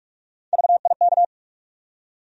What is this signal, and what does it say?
Morse code, heard as a keyed tone, about 42 wpm, 700 Hz. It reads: VIK
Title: USA Call Sign Postfixes - 3 Letters 45wpm